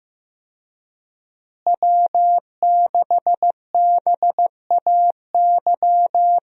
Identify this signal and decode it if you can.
Morse code, heard as a keyed tone, about 15 wpm, 700 Hz. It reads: W6BAY
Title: USA Call Signs - 5 Characters - Repeated 2x - 15wpm